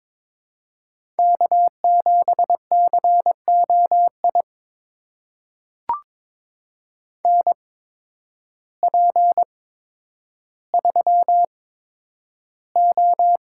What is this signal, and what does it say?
Morse code, heard as a keyed tone, about 22 wpm, spaced as for 10 wpm, 700 Hz. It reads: K7COI NP3O
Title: USA Call Signs - 8x Character Spacing - 22wpm